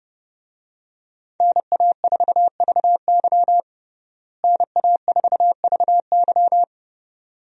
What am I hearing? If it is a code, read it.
Morse code, heard as a keyed tone, about 30 wpm, 700 Hz. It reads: NA4VY NA4VY
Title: USA Call Signs - 5 Characters - Repeated 2x - 30wpm